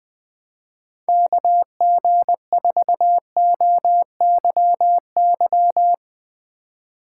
Morse code, heard as a keyed tone, about 20 wpm, 700 Hz. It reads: KG4OYY